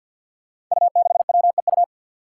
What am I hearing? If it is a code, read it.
Morse code, heard as a keyed tone, about 52 wpm, 700 Hz. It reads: U6WEU
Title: Five Challenging Letter-Number Combinations 50wpm